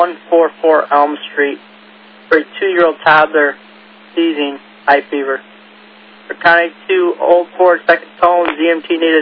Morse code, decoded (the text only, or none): none